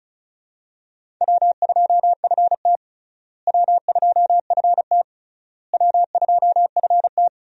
Morse code, heard as a keyed tone, about 36 wpm, 700 Hz. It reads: W2FT W2FT W2FT
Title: USA Call Signs - 4 Characters - Repeated 3x - 35wpm